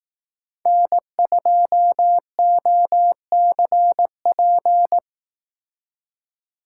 Morse code, heard as a keyed tone, about 18 wpm, 700 Hz. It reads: N2OCP
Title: Speed Racing - USA Call Signs 15wpm